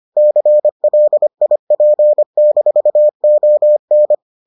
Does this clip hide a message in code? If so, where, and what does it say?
Yes: Morse code from 0.2 to 4.1 s, heard as a keyed tone, about 25 wpm, 600 Hz. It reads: CLIP-ON